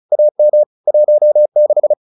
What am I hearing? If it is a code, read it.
Morse code, heard as a keyed tone, about 34 wpm, 600 Hz. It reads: AM 16